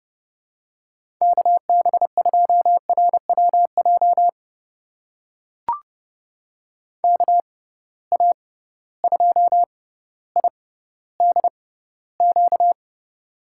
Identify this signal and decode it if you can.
Morse code, heard as a keyed tone, about 30 wpm, spaced as for 15 wpm, 700 Hz. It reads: KB2RWJ KA2IDQ
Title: USA Call Signs - 6 Characters - 6x Character Spacing - 30wpm